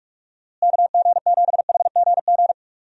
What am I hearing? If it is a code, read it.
Morse code, heard as a keyed tone, about 46 wpm, 700 Hz. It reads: KG7HGG
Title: USA Call Signs - 6 Characters - Repeated 3x - 45wpm